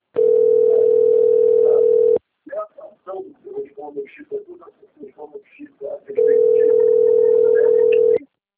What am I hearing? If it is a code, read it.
Ringback tone (North America), 440 + 480 Hz, with speech.
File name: rg-1600-unknown- - - .24962